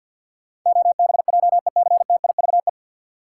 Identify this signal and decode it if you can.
Morse code, heard as a keyed tone, about 51 wpm, 700 Hz. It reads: OBJECTIVE
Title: Sentences from Top 1200 Words - New Word Drill - 50wpm